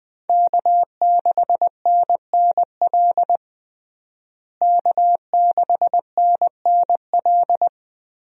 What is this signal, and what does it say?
Morse code, heard as a keyed tone, about 20 wpm, 700 Hz. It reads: K6NNL K6NNL
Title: USA Call Signs - 5 Characters - Repeated 3x - 20wpm